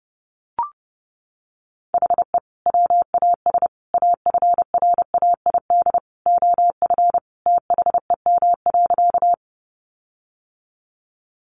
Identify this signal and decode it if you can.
Morse code, heard as a keyed tone, about 30 wpm, 700 Hz. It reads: HE WAS AFRAID OF THEM.
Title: Sentences from Top 1500 Words 30wpm